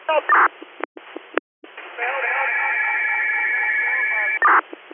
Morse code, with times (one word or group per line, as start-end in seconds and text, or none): none